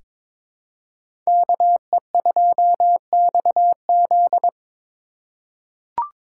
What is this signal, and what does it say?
Morse code, heard as a keyed tone, about 22 wpm, 700 Hz. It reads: KE2XZ
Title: USA Call Signs - 5 Characters - 8x Character Spacing - 22wpm